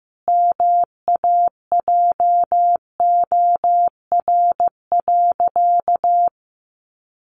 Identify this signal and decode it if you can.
Morse code, heard as a keyed tone, about 15 wpm, 700 Hz. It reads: MAJOR.
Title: Sentences from Top 600 Words 15wpm